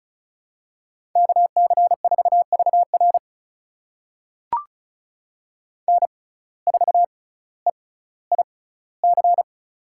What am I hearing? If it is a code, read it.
Morse code, heard as a keyed tone, about 36 wpm, spaced as for 18 wpm, 700 Hz. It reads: KC4VR N4EIC
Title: USA Call Signs - 5 Characters - 6x Character Spacing - 35wpm